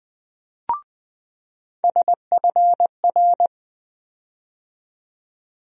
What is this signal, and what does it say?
Morse code, heard as a keyed tone, about 20 wpm, 700 Hz. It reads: SFR
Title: USA Call Sign Postfixes - 3 Letters 20wpm